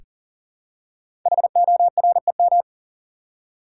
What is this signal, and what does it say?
Morse code, heard as a keyed tone, about 42 wpm, 700 Hz. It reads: HOREM
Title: Code Groups - 5 Letters - 40wpm